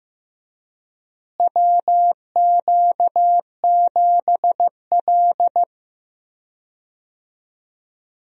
Morse code, heard as a keyed tone, about 15 wpm, 700 Hz. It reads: WQ7L
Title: USA Call Signs - 4 Characters - Repeated 2x - 15wpm